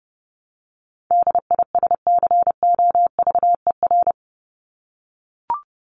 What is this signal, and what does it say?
Morse code, heard as a keyed tone, about 29 wpm, 700 Hz. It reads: DISCOVER
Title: Top 1001-1250 Words 30wpm